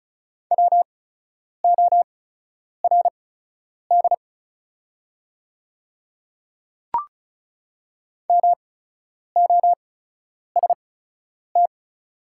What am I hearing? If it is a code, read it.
Morse code, heard as a keyed tone, about 36 wpm, spaced as for 10 wpm, 700 Hz. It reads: WORD MOST
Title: 8x Character Spacing - Top 100 Words - 35wpm